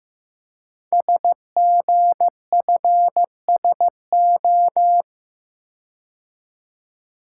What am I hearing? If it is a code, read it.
Morse code, heard as a keyed tone, about 15 wpm, 690 Hz. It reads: SGFSO
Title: Code Groups - 5 Letters - 15wpm